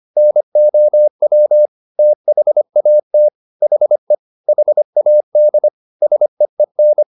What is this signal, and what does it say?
Morse code, heard as a keyed tone, about 25 wpm, 600 Hz. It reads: NOW THAT HE HAD SEEN
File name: the old man and the sea part 1 25wpm